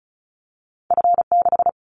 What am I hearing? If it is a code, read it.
Morse code, heard as a keyed tone, about 34 wpm, 700 Hz. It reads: F6
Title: Callsign Prefixes for Top 25 DXCC - Repeated 3x - 35wpm